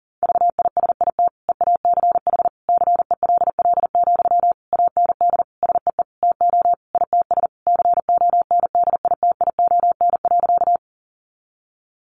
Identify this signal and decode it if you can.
Morse code, heard as a keyed tone, about 40 wpm, 700 Hz. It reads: VISIT EACH CELL, AND SEE TO ITS CONDITION.
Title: Sentences from Top 1100 Words 40wpm